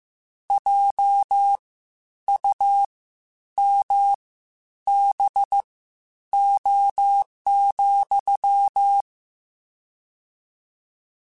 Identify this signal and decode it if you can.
Morse code, heard as a keyed tone, about 15 wpm, spaced as for 9 wpm, 800 Hz. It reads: JUMBO,
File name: Misc / Misc-mp3s / WordPairs sp @13